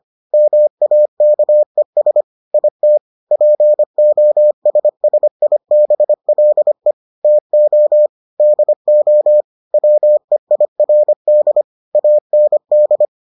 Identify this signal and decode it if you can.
Morse code, heard as a keyed tone, about 25 wpm, 600 Hz. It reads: MAKES IT POSSIBLE TO DO WEIRD AND